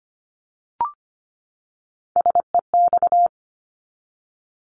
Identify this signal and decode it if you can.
Morse code, heard as a keyed tone, about 25 wpm, 700 Hz. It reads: SEX
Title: Top 1501-1750 Words 25wpm